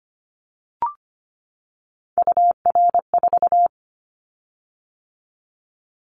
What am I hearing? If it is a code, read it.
Morse code, heard as a keyed tone, about 25 wpm, 700 Hz. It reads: UR4
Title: Callsign Prefixes for Top 25 DXCC - 25wpm